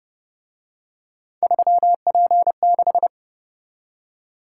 Morse code, heard as a keyed tone, about 29 wpm, 700 Hz. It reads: 3P6